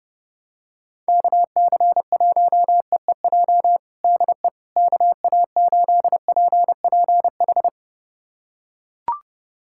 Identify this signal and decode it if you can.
Morse code, heard as a keyed tone, about 30 wpm, 700 Hz. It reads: KC1EEJ DE KA8PPH
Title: USA Callsign DE Callsign - 6 Characters - 30wpm